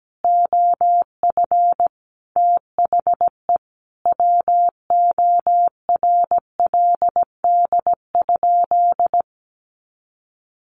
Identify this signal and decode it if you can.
Morse code, heard as a keyed tone, about 17 wpm, 700 Hz. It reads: OF THE WORLD?
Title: Sentences from Top 200 Words 17wpm